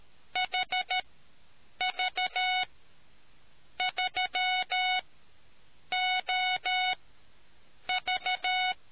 Morse code, heard as a keyed tone, about 13 wpm, spaced as for 7 wpm, 740 Hz. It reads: HV3OV